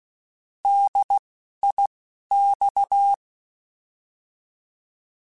Morse code, heard as a keyed tone, about 16 wpm, 800 Hz. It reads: D I X
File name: Alphabetical / Alphabetical mp3s / Ds @10